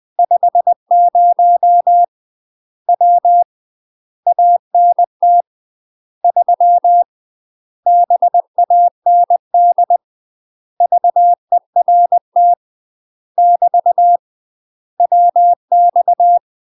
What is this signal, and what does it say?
Morse code, heard as a keyed tone, about 20 wpm, 700 Hz. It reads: 50 W ANT 3 BAND VERT <BT> WX
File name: UK MASTER QSOs TRAINING / 20WPM / UK MASTER QSOs 0021